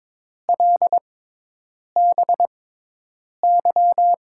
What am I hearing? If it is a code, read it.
Morse code, heard as a keyed tone, about 22 wpm, spaced as for 8 wpm, 700 Hz. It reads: LBY
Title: USA Call Signs - 6 Characters - 6x Character Spacing - 22wpm